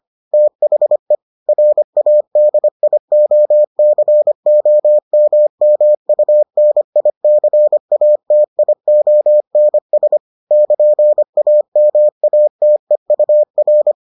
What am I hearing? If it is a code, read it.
Morse code, heard as a keyed tone, about 25 wpm, 600 Hz. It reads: THE RADIOCOMMUNICATIONS <KN>AMATEUR